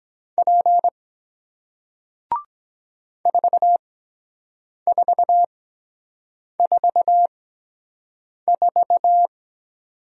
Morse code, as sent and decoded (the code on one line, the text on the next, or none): .--. / ....- ....- ....- ....-
P 4444